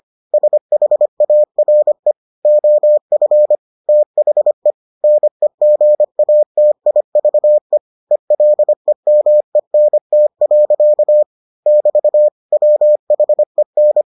Morse code, heard as a keyed tone, about 25 wpm, 600 Hz. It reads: SHARE OF THE NEGATIVE ELEMENT. <BT> WHEN